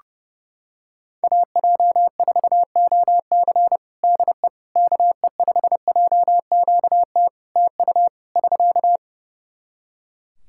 Morse code, heard as a keyed tone, about 32 wpm, 700 Hz. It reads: AJ4OC DE KE5JQT TU <SK>
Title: K1USN SST QSO Practice 30wpm